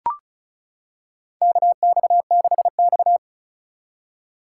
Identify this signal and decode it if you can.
Morse code, heard as a keyed tone, about 34 wpm, 700 Hz. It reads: KX6X
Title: K1USN SST QSO Practice - 35wpm